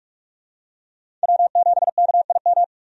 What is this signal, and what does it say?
Morse code, heard as a keyed tone, about 47 wpm, 700 Hz. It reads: W7KIM
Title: USA Call Signs - 5 Characters - 45wpm